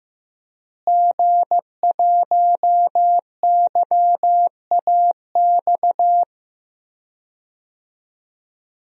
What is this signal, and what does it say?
Morse code, heard as a keyed tone, about 15 wpm, 700 Hz. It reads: G1YAX
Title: DXCC-100 Call Signs - 5 Characters - Repeated 3x - 15wpm